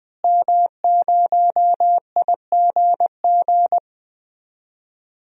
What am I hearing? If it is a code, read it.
Morse code, heard as a keyed tone, about 20 wpm, 700 Hz. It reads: M0IGG
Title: DXCC-100 Call Signs - 5 Characters - Repeated 2x - 20wpm